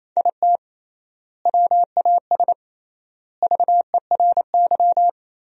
Morse code, heard as a keyed tone, about 28 wpm, 700 Hz. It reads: IT WAS VERY
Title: 3x Word-Spacing - Sentences from Top 500 Words - 28wpm